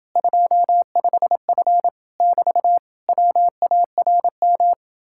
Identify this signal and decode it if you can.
Morse code, heard as a keyed tone, about 31 wpm, 700 Hz. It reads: 25F <BT> WARM